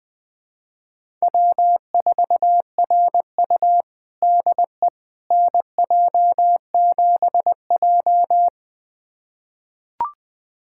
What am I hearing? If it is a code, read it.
Morse code, heard as a keyed tone, about 20 wpm, 700 Hz. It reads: W4RU DE NJ7J
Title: USA Callsign DE Callsign - 4 Characters - 20wpm